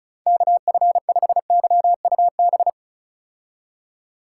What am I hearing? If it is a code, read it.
Morse code, heard as a keyed tone, about 34 wpm, 700 Hz. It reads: KF5YUB
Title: K1USN SST QSO Practice - 35wpm